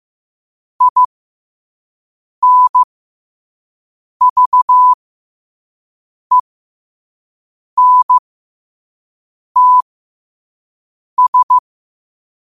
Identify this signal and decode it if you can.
Morse code, heard as a keyed tone, about 15 wpm, spaced as for 5 wpm, 1000 Hz. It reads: INVENTS
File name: qotdfiles / 5wpm